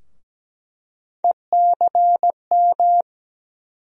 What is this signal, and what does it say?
Morse code, heard as a keyed tone, about 17 wpm, 700 Hz. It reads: ECM
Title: USA Call Sign Postfixes - 3 Letters 17wpm